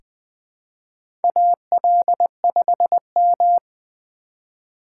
Morse code, heard as a keyed tone, about 20 wpm, 700 Hz. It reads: AL5M
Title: USA Call Signs - 4 Characters - 2x Character Spacing - 20wpm